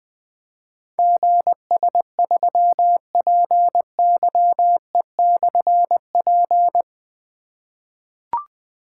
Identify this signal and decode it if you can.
Morse code, heard as a keyed tone, about 20 wpm, 700 Hz. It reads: GS3PYE/P
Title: Difficult Call Signs 20wpm